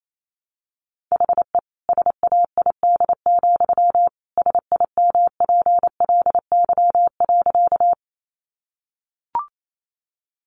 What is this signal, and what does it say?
Morse code, heard as a keyed tone, about 28 wpm, 700 Hz. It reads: HE SAID, SIMPLY.
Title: Sentences from Top 600 Words 28wpm